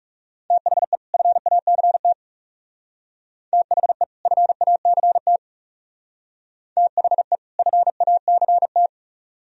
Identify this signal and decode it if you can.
Morse code, heard as a keyed tone, about 38 wpm, 700 Hz. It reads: THE FACT THE FACT THE FACT